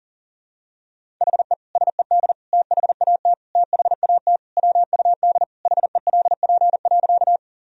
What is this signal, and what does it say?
Morse code, heard as a keyed tone, about 40 wpm, 700 Hz. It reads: HE SED THAT THAT WUD HELP.
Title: Sentences from Top 200 Words and Abbreviations 40wpm